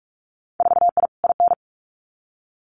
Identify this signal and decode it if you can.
Morse code, heard as a keyed tone, about 44 wpm, 700 Hz. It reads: 4I IN
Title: Winter Field Day Exchanges Encore 45wpm